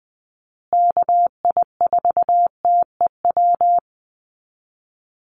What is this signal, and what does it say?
Morse code, heard as a keyed tone, about 20 wpm, 700 Hz. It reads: KI4TEW
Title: USA Call Signs Repeated 3x 20wpm